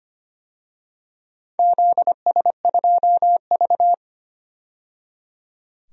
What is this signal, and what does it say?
Morse code, heard as a keyed tone, about 25 wpm, 700 Hz. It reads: ZS2V
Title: DXCC-100 Call Signs - 4 Characters - 25wpm